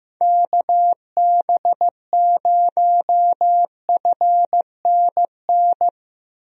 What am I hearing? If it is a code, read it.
Morse code, heard as a keyed tone, about 15 wpm, 700 Hz. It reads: KB0FNN
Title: USA Call Signs - 6 Characters - Repeated 2x - 15wpm